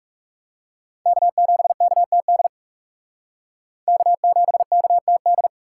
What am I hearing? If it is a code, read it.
Morse code, heard as a keyed tone, about 36 wpm, 700 Hz. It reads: K7KTD K7KTD
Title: Speed Racing - USA 1x3 Call Signs 30wpm